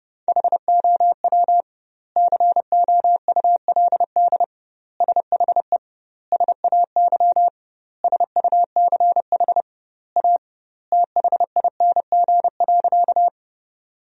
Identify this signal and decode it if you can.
Morse code, heard as a keyed tone, about 30 wpm, 700 Hz. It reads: HOW COULD SHE SAY SUCH A THING.
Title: 2x Word-Spacing - Sentences from Top 200 Words - 30wpm